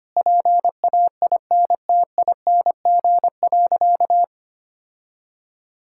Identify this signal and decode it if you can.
Morse code, heard as a keyed tone, about 25 wpm, 700 Hz. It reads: PAINTING.
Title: Sentences from Top 1000 Words - Review Part 6 - 25wpm